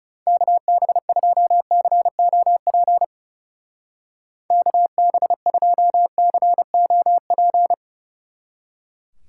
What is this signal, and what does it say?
Morse code, heard as a keyed tone, about 37 wpm, 700 Hz. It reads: KB2COP KB2COP
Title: Speed Racing - USA 2x3 Call Signs 30wpm